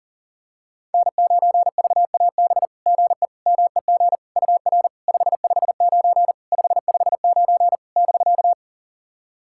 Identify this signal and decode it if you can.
Morse code, heard as a keyed tone, about 40 wpm, 700 Hz. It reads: N9VAB GE MEG UR 559 559 <BK>
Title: SOTA QSO Practice 40wpm